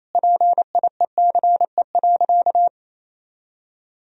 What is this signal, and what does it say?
Morse code, heard as a keyed tone, about 28 wpm, 700 Hz. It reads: PIECE.
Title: Sentences from Top 1000 Words - Review Part 3 - 28wpm